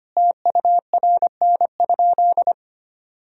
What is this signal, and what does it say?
Morse code, heard as a keyed tone, about 25 wpm, 700 Hz. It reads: TURN?
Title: Sentences from Top 1600 Words 25wpm